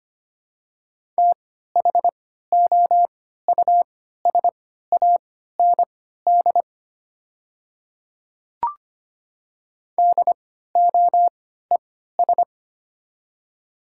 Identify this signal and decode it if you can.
Morse code, heard as a keyed tone, about 25 wpm, spaced as for 14 wpm, 700 Hz. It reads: THOUSAND DOES
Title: 3x Character Spacing - Top 500 Words - 25wpm